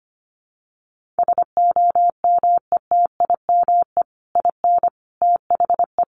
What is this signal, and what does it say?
Morse code, heard as a keyed tone, about 25 wpm, 700 Hz. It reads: SOMETIME IN THE
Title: Sentences from Top 1500 Words 25wpm